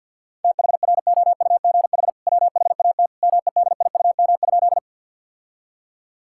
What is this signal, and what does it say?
Morse code, heard as a keyed tone, about 50 wpm, 700 Hz. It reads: THROUGH WHAT MEDIUM?
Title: Sentences from Top 1000 Words - Review Part 6 - 50wpm